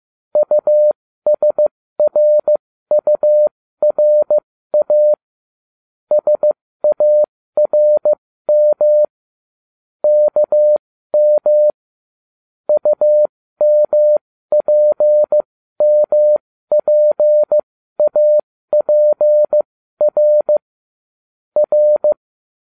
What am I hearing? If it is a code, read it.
Morse code, heard as a keyed tone, about 15 wpm, spaced as for 13 wpm, 600 Hz. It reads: USRURA SARM KM UMPMPAPR R